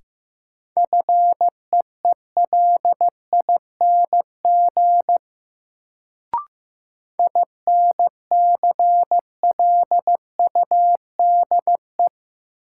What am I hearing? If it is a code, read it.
Morse code, heard as a keyed tone, about 15 wpm, 700 Hz. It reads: FEELING INCLUDE
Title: Sentences from Top 1000 Words - Review Part 3 - New Word Drill - 15wpm